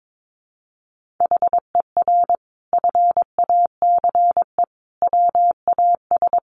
Morse code, heard as a keyed tone, about 22 wpm, 700 Hz. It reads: HER FACE WAS
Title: Sentences from Top 1500 Words 22wpm